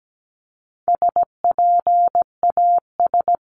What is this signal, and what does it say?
Morse code, heard as a keyed tone, about 17 wpm, 700 Hz. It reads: SPAS
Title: CWT Exchange 17wpm